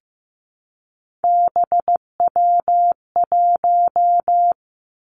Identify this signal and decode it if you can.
Morse code, heard as a keyed tone, about 15 wpm, 700 Hz. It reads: BW1
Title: Three Challenging Letter-Number Combinations 15wpm